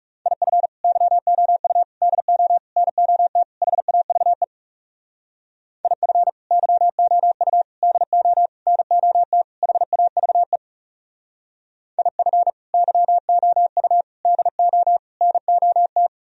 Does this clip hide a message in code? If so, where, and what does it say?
Yes: Morse code from 0.3 to 16.1 s, heard as a keyed tone, about 40 wpm, 700 Hz. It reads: IF YOU DO NOT HAVE IF YOU DO NOT HAVE IF YOU DO NOT